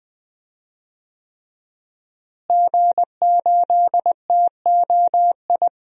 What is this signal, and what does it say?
Morse code, heard as a keyed tone, about 20 wpm, 700 Hz. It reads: G8TOI
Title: DXCC-100 Call Signs - 5 Characters - Repeated 3x - 20wpm